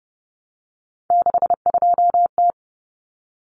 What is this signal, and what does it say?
Morse code, heard as a keyed tone, about 30 wpm, 700 Hz. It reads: 62T
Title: Common Cut Numbers - 3-Digit 30wpm